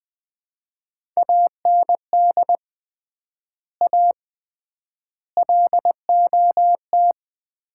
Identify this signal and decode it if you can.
Morse code, heard as a keyed tone, about 20 wpm, 700 Hz. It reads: AND A LOT
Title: 3x Word-Spacing - Sets of 4 Words - 20wpm